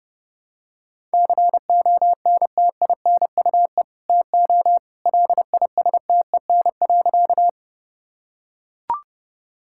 Morse code, heard as a keyed tone, about 28 wpm, 700 Hz. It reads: CONTINUE TO LISTEN.
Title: Sentences from Top 1000 Words - Review Part 6 - 30wpm